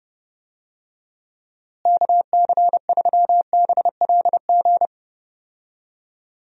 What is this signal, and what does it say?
Morse code, heard as a keyed tone, about 30 wpm, 700 Hz. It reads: KC3BLG
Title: USA Call Signs - 6 Characters - Repeated 3x - 30wpm